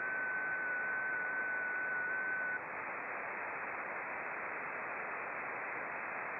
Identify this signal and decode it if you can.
FT8.
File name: contests / sept-vhf-k5tr- -0300